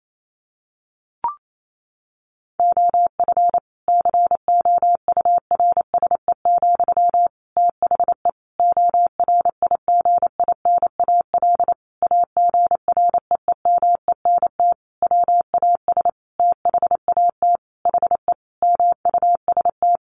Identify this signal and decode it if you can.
Morse code, heard as a keyed tone, about 28 wpm, 700 Hz. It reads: OF COURSE, THE ORIGINAL AGREEMENT WAS THAT HE MUST
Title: Sentences from Top 1500 Words 28wpm